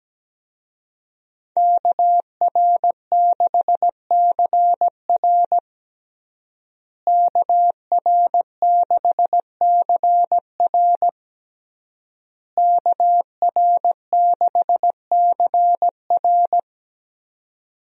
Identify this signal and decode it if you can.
Morse code, heard as a keyed tone, about 17 wpm, 700 Hz. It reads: KR6CR KR6CR KR6CR
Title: USA Call Signs - 5 Characters - Repeated 3x - 17wpm